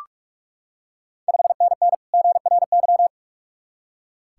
Morse code, heard as a keyed tone, about 44 wpm, 690 Hz. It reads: 5NN GRY
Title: Ontario QSO Party Exchange 45wpm